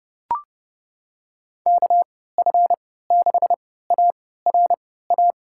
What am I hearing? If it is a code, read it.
Morse code, heard as a keyed tone, about 29 wpm, spaced as for 17 wpm, 700 Hz. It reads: KF6ARA
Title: USA Call Signs - 6 Characters - 3x Character Spacing - 30wpm